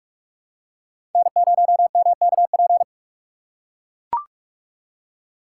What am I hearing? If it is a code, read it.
Morse code, heard as a keyed tone, about 44 wpm, 700 Hz. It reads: N0MKP